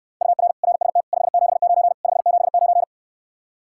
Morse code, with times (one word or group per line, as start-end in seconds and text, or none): 0.2-0.5 UR
0.6-1.0 RST
1.1-1.9 579
2.0-2.8 579